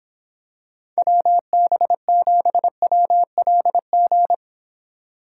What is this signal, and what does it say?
Morse code, heard as a keyed tone, about 26 wpm, 700 Hz. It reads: WB7WLG